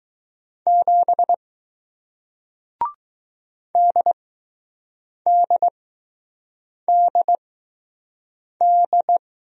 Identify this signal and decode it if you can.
Morse code, heard as a keyed tone, about 20 wpm, spaced as for 6 wpm, 700 Hz. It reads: 7 DDDD